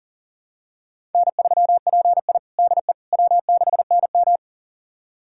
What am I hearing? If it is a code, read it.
Morse code, heard as a keyed tone, about 37 wpm, 700 Hz. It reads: N3PI DE W6NM
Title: USA Callsign DE Callsign - 4 Characters - 40wpm